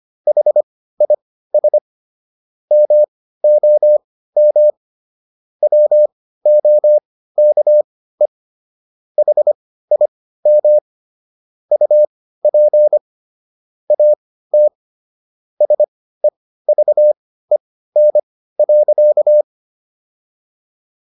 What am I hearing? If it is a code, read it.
Morse code, heard as a keyed tone, about 25 wpm, spaced as for 15 wpm, 600 Hz. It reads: HIS MOM WOKE HIM UP AT SEVEN.